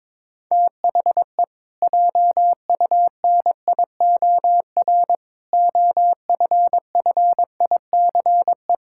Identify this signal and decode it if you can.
Morse code, heard as a keyed tone, about 22 wpm, 700 Hz. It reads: THE JUNIOR OFFICE
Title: Pangram Challenge v2 - 22wpm